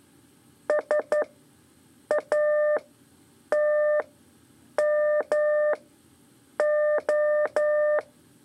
Morse code, heard as a keyed tone, about 12 wpm, spaced as for 7 wpm, 600 Hz. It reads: SATMO
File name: 3 lezione CW 2 AIR